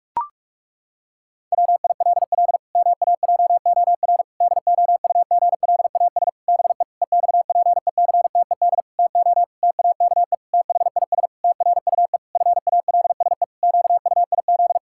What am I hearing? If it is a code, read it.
Morse code, heard as a keyed tone, about 46 wpm, 700 Hz. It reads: WILL MAJOR DOUGLAS BE EXPECTED TO TAKE THIS TRUE FALSE QUIZ
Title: Pangram Challenge v2 - 45wpm